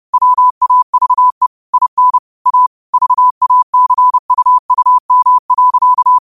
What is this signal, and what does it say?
Morse code, heard as a keyed tone, about 30 wpm, 1000 Hz. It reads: WAVE IN A VACUUM.
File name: qotdfiles / 30wpm